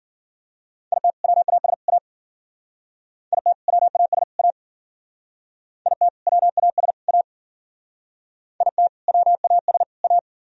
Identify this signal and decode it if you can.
Morse code, heard as a keyed tone, about 53 wpm, 700 Hz. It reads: IT WAS A IT WAS A IT WAS A IT WAS A